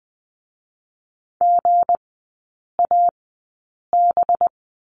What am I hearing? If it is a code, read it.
Morse code, heard as a keyed tone, about 20 wpm, spaced as for 8 wpm, 700 Hz. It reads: GAB
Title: 184 Lesson 30e - Letter b - Three Characters 20wpm